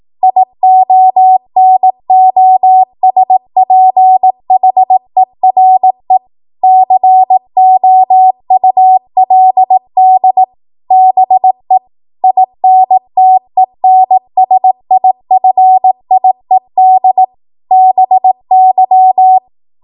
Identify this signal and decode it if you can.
Morse code, heard as a keyed tone, about 18 wpm, 750 Hz. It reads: IONOSPHERE COULD BE INTENSIFIED BY
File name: Morse / 18WPM